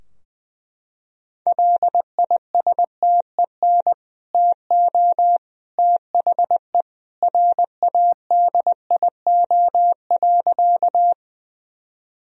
Morse code, heard as a keyed tone, about 20 wpm, 700 Hz. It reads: LISTEN TO THE RADIO.
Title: Sentences from Top 1000 Words - Review Part 3 - 20wpm